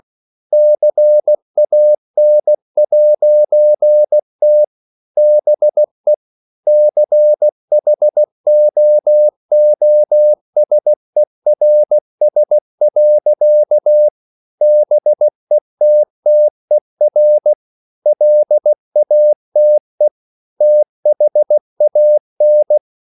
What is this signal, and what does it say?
Morse code, heard as a keyed tone, about 16 wpm, 600 Hz. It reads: CAN'T BE CHOOSERS. BETTER LATE THAN
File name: CW / Practice / EnglishSayings16